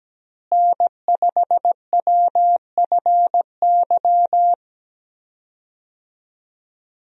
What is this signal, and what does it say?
Morse code, heard as a keyed tone, about 17 wpm, 700 Hz. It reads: N5WFY
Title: USA Call Signs - 5 Characters - Repeated 3x - 17wpm